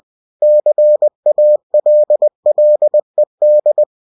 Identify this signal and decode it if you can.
Morse code, heard as a keyed tone, about 20 wpm, 600 Hz. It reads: CALLED
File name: winnie-20wpm